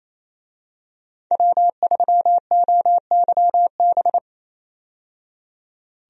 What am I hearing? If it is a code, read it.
Morse code, heard as a keyed tone, about 27 wpm, 700 Hz. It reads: W3OYB
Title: USA Call Signs - 5 Characters - 28wpm